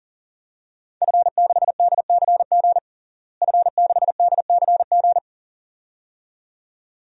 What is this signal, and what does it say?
Morse code, heard as a keyed tone, about 40 wpm, 700 Hz. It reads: F6DCG F6DCG
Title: DXCC-100 Call Signs - 5 Characters - Repeated 2x - 40wpm